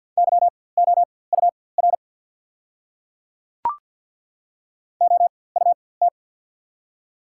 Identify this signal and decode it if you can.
Morse code, heard as a keyed tone, about 43 wpm, spaced as for 22 wpm, 690 Hz. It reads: YOUR OUT
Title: 4x Character Spacing - Top 100 Words - 50wpm